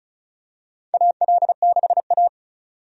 Morse code, heard as a keyed tone, about 34 wpm, 700 Hz. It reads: AL6A
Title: USA Call Signs - 4 Characters - 35wpm